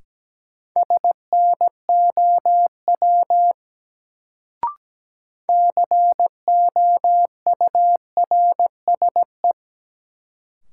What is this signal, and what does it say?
Morse code, heard as a keyed tone, about 17 wpm, 700 Hz. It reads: SNOW COURSE
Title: Sentences from Top 500 Words - New Word Drill - 17wpm